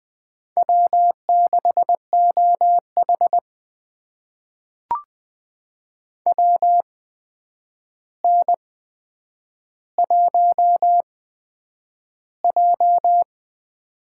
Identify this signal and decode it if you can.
Morse code, heard as a keyed tone, about 20 wpm, spaced as for 9 wpm, 700 Hz. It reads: W6OH WN1J
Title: USA Call Signs - 4 Characters - 8x Character Spacing - 20wpm